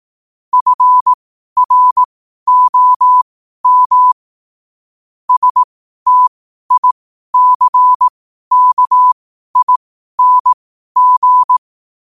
Morse code, heard as a keyed tone, about 18 wpm, spaced as for 12 wpm, 1000 Hz. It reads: FROM STICKING
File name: qotdfiles / 12wpm